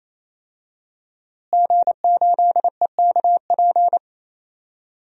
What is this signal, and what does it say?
Morse code, heard as a keyed tone, about 28 wpm, 700 Hz. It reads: G8EKP
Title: DXCC-100 Call Signs - 5 Characters - Repeated 3x - 28wpm